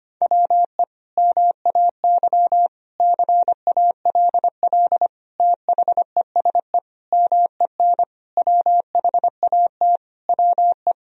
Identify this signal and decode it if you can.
Morse code, heard as a keyed tone, about 25 wpm, 700 Hz. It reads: WE MAY CALL THESE MEN WHAT WE
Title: Sentences from Top 1000 Words - Review Part 1 - 25wpm